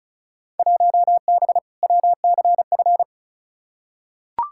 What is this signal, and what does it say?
Morse code, heard as a keyed tone, about 34 wpm, 700 Hz. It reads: 1B WCF
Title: Field Day Exchanges Encore - 35wpm